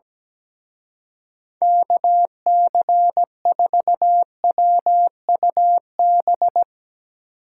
Morse code, heard as a keyed tone, about 17 wpm, 700 Hz. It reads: KC4WUB
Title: Speed Racing - USA 2x3 Call Signs 17wpm